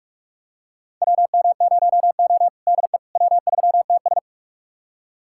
Morse code, heard as a keyed tone, about 46 wpm, 700 Hz. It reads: WM0O DE W3TS
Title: USA Callsign DE Callsign - 4 Characters - 45wpm